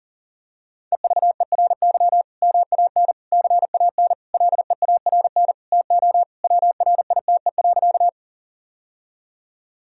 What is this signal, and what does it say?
Morse code, heard as a keyed tone, about 40 wpm, 700 Hz. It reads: EVERY MAN CAN LEARN TO WRITE.
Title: Sentences from Top 1000 Words - Review Part 1 - 40wpm